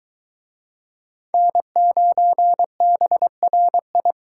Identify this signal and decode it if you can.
Morse code, heard as a keyed tone, about 24 wpm, 700 Hz. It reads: N9BRI